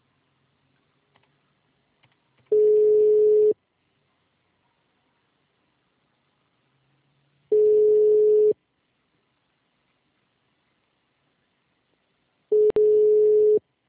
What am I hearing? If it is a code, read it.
Ringback tone (Europe, 425 Hz).